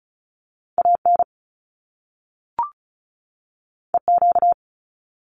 Morse code, heard as a keyed tone, about 36 wpm, 700 Hz. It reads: AN EQ
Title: Top Two Letter Combinations in Words 35wpm